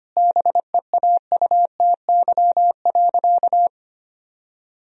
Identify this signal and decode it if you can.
Morse code, heard as a keyed tone, about 25 wpm, 700 Hz. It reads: BEAUTY.
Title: Sentences from Top 1000 Words - Review Part 2 - 25wpm